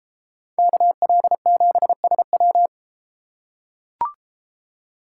Morse code, heard as a keyed tone, about 31 wpm, 700 Hz. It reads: KL7SW